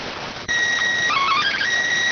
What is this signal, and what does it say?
SSTV (slow-scan television), PD 290.